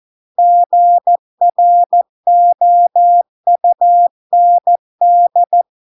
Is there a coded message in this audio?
Yes: Morse code from 0.4 to 5.6 s, heard as a keyed tone, about 14 wpm, 700 Hz. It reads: GROUND